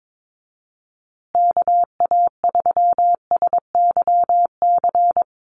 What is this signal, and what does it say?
Morse code, heard as a keyed tone, about 22 wpm, 700 Hz. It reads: KA3SYC
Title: USA Call Signs Repeated 2x 22wpm